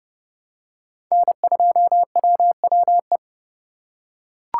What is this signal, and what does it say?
Morse code, heard as a keyed tone, about 30 wpm, 700 Hz. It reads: N2WWE
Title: USA Call Signs - 5 Characters - 4x Character Spacing - 30wpm